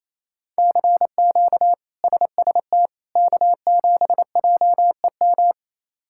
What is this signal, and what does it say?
Morse code, heard as a keyed tone, about 27 wpm, 700 Hz. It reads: CQ SST K7JEM
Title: K1USN SST QSO Practice - 28wpm